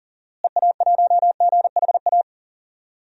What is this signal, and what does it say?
Morse code, heard as a keyed tone, about 40 wpm, 700 Hz. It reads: EA1GHA
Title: DXCC-100 Call Signs - 6 Characters - Repeated 3x - 40wpm